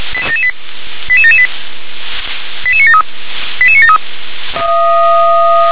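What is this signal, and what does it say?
Five-tone selective call (ZVEI), heard as quick runs of five tones, tones of about 70 ms. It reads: 88778 88973 88973
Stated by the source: ZVEI five-tone selective call received by SDR.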